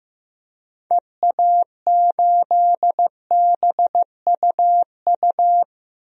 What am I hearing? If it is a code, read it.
Morse code, heard as a keyed tone, about 16 wpm, 700 Hz. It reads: EA8BUU
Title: DXCC-100 Call Signs - 6 Characters - 15wpm